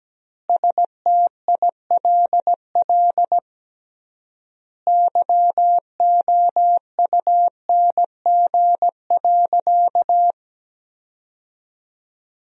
Morse code, heard as a keyed tone, about 17 wpm, 700 Hz. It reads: STILL YOUNG.
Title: 3x Word-Spacing - Sentences from Top 400 Words - 17wpm